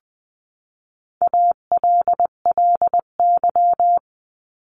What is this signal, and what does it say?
Morse code, heard as a keyed tone, about 20 wpm, 700 Hz. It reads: ALLY
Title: 259 Lesson 42d - Words 18-20wpm